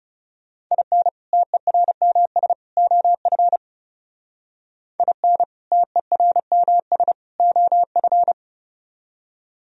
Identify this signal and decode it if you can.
Morse code, heard as a keyed tone, about 39 wpm, 700 Hz. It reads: IN TERMS OF IN TERMS OF